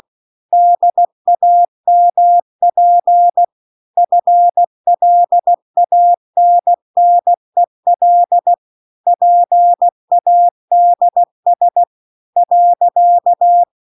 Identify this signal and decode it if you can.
Morse code, heard as a keyed tone, about 16 wpm, 700 Hz. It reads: DAMP FLANNEL PADS .